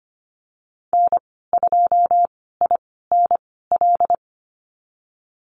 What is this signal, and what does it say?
Morse code, heard as a keyed tone, about 25 wpm, spaced as for 16 wpm, 700 Hz. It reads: N2INL
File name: Lesson 15g - Number 5 - Call Signs 25wpm